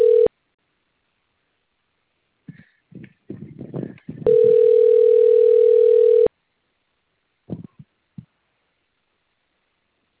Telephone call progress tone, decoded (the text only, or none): ringback tone